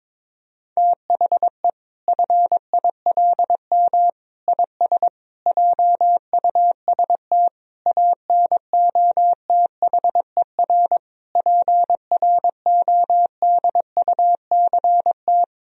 Morse code, heard as a keyed tone, about 22 wpm, 700 Hz. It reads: THE FILM IS JUST ANOTHER PRODUCT
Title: Sentences from Top 1000 Words - Review Part 3 - 22wpm